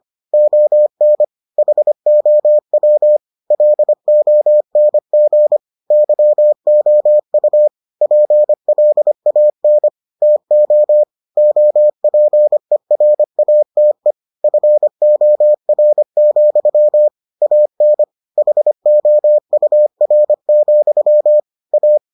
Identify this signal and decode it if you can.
Morse code, heard as a keyed tone, about 25 wpm, 600 Hz. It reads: ON HOW LONG YOU PLAN TO OPERATE FOR, AN HOUR, A